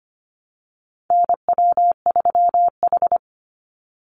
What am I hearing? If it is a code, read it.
Morse code, heard as a keyed tone, about 25 wpm, 700 Hz. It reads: NW3H
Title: USA Call Signs 25wpm